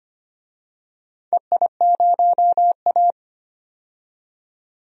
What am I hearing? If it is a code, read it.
Morse code, heard as a keyed tone, about 25 wpm, 700 Hz. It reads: EI0A
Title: DXCC-100 Call Signs - 4 Characters - 25wpm